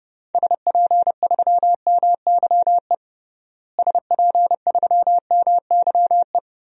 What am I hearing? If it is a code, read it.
Morse code, heard as a keyed tone, about 30 wpm, 700 Hz. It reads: SP3MYE SP3MYE
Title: DXCC-100 Call Signs - 6 Characters - Repeated 3x - 30wpm